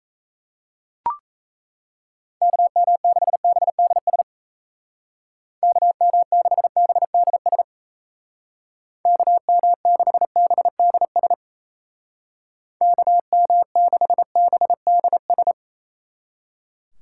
Morse code, heard as a keyed tone, about 35 wpm, 700 Hz. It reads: KM6BDS KM6BDS KM6BDS KM6BDS
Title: Speed Racing - USA 2x3 Call Signs 28wpm